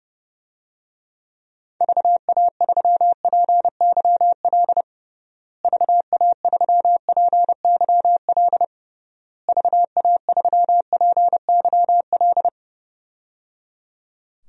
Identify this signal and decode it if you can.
Morse code, heard as a keyed tone, about 30 wpm, 700 Hz. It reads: VA3PYL VA3PYL VA3PYL
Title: DXCC-100 Call Signs - 6 Characters - Repeated 3x - 30wpm